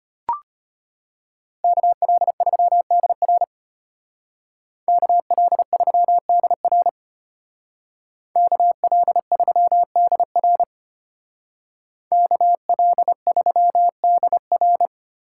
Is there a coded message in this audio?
Yes: Morse code from 1.6 to 14.9 s, heard as a keyed tone, about 33 wpm, 700 Hz. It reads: KL3DR KL3DR KL3DR KL3DR